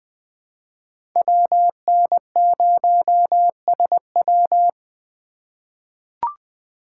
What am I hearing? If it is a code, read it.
Morse code, heard as a keyed tone, about 20 wpm, 700 Hz. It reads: WN0SW
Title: USA Call Signs - 5 Characters - 3x Character Spacing - 20wpm